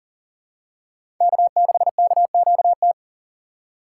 Morse code, heard as a keyed tone, about 40 wpm, 700 Hz. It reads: K6KQT
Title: USA Call Signs - 5 Characters - 3x Character Spacing - 40wpm